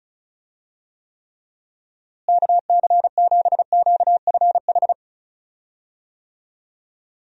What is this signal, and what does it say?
Morse code, heard as a keyed tone, about 36 wpm, 700 Hz. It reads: KC7QFH
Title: USA Call Signs - 6 Characters - Repeated 3x - 35wpm